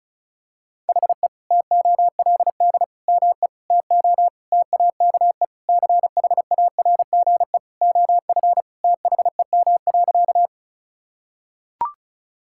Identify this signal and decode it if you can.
Morse code, heard as a keyed tone, about 36 wpm, 700 Hz. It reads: HE TOLD ME TO TAKE CHARGE OF THEM.
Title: Sentences from Top 1000 Words - Review Part 6 - 35wpm